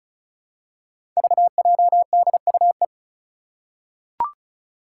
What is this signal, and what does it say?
Morse code, heard as a keyed tone, about 36 wpm, 700 Hz. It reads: VJDUE
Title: Five Challenging Letter-Number Combinations 35wpm